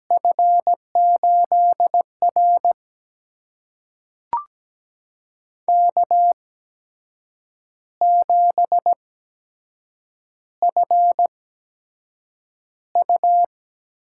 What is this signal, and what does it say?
Morse code, heard as a keyed tone, about 17 wpm, spaced as for 7 wpm, 700 Hz. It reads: F8R K7FU
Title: USA Call Signs - 4 Characters - 8x Character Spacing - 17wpm